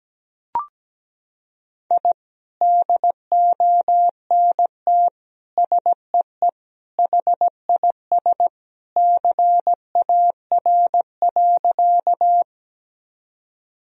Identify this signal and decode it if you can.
Morse code, heard as a keyed tone, about 17 wpm, 700 Hz. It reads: I DONT SEE HIS CAR.
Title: Sentences from Top 1000 Words - Review Part 1 - 17wpm